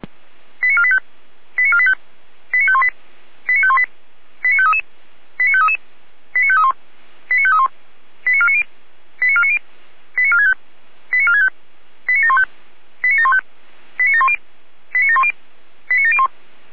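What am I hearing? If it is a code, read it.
Five-tone selective call (ZVEI), heard as quick runs of five tones, tones of about 70 ms. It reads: 89485 89485 89518 89518 89533 89533 89531 89531 89490 89490 89465 89465 89715 89715 89710 89710 89701
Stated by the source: Five-tone ZVEI selective call received on SDR, from the feuerwehr-orlamuende.org fire brigade audio archive.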